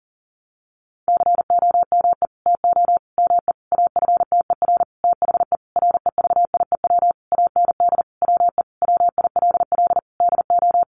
Morse code, heard as a keyed tone, about 40 wpm, 700 Hz. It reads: COME TO ME AFTER THE REVIEW AND WE WILL DO
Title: Sentences from Top 900 Words 40wpm